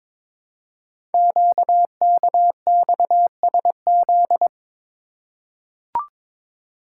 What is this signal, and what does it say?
Morse code, heard as a keyed tone, about 22 wpm, 700 Hz. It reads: QKXSZ
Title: Code Groups - 5 Letters - 22wpm